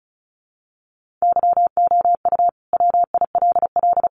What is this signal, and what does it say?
Morse code, heard as a keyed tone, about 34 wpm, 700 Hz. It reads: YOU WILL
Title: Sentences from Top 1500 Words 35wpm